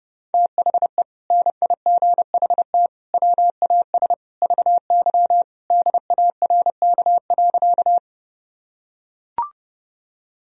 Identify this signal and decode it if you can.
Morse code, heard as a keyed tone, about 28 wpm, 700 Hz. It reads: THE NIGHT WAS VY DARK.
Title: Sentences from Top 500 Words and Abbreviations 30wpm